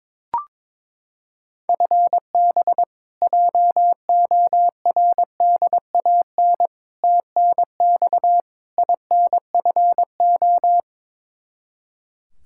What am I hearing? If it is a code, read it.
Morse code, heard as a keyed tone, about 22 wpm, 700 Hz. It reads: FB JORDAN TNX INFO
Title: Practice QSO - Exchange 3 - More Info - 22wpm